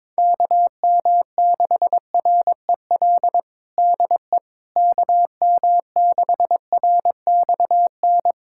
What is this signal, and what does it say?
Morse code, heard as a keyed tone, about 22 wpm, 700 Hz. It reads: KM6REL DE KM6RXN
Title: USA Callsign DE Callsign - 6 Characters - 22wpm